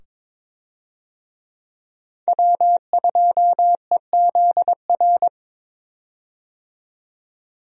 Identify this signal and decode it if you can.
Morse code, heard as a keyed tone, about 22 wpm, 700 Hz. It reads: W2EZR
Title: USA Call Signs - 5 Characters - 22wpm